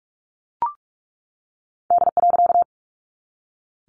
Morse code, heard as a keyed tone, about 45 wpm, 700 Hz. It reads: D.
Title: A Sound Of Thunder 45wpm